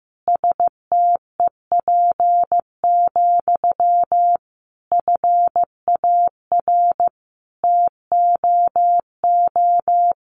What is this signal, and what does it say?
Morse code, heard as a keyed tone, about 15 wpm, 700 Hz. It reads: STEP, FAR TOO
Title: A Sound Of Thunder 15wpm